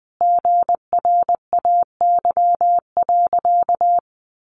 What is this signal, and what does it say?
Morse code, heard as a keyed tone, about 20 wpm, 700 Hz. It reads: GRAY.
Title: Additional Easy Phrases 20wpm